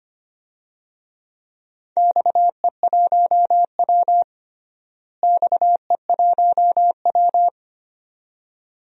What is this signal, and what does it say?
Morse code, heard as a keyed tone, about 25 wpm, 700 Hz. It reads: XE1W XE1W
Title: DXCC-100 Call Signs - 4 Characters - Repeated 2x - 25wpm